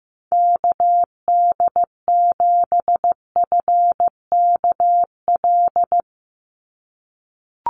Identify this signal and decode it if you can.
Morse code, heard as a keyed tone, about 15 wpm, 700 Hz. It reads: KD7FKL
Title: USA Call Signs 15wpm